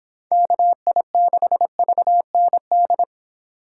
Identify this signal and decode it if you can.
Morse code, heard as a keyed tone, about 26 wpm, 700 Hz. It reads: KI6VND